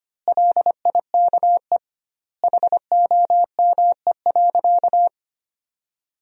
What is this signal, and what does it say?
Morse code, heard as a keyed tone, about 25 wpm, 700 Hz. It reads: LIKE HOME.
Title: 2x Word-Spacing - Sentences from Top 200 Words - 25wpm